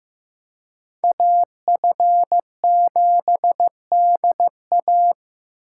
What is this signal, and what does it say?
Morse code, heard as a keyed tone, about 18 wpm, 700 Hz. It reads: AF7DA
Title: Speed Racing - USA Call Signs 15wpm